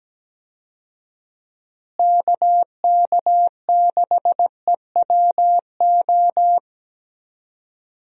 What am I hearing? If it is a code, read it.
Morse code, heard as a keyed tone, about 17 wpm, 700 Hz. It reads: KK6EWO
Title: USA Call Signs - 6 Characters - Repeated 2x - 17wpm